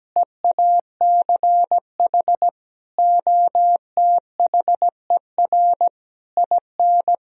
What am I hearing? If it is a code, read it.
Morse code, heard as a keyed tone, about 17 wpm, 700 Hz. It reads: EACH OTHER IN
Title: Sentences from Top 1000 Words - Review Part 1 - 17wpm